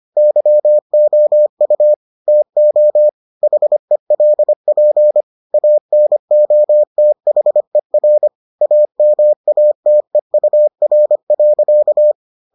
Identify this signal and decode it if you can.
Morse code, heard as a keyed tone, about 25 wpm, 600 Hz. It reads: YOU TO HELP ANOTHER AMATEUR.